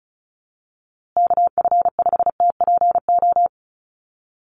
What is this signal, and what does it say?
Morse code, heard as a keyed tone, about 34 wpm, 700 Hz. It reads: KF5TPO
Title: USA Call Signs Repeated 2x 35wpm